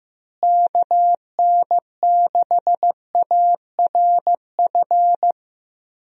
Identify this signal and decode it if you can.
Morse code, heard as a keyed tone, about 15 wpm, 700 Hz. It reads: KN6ARF
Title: USA Call Signs - 6 Characters - Repeated 3x - 15wpm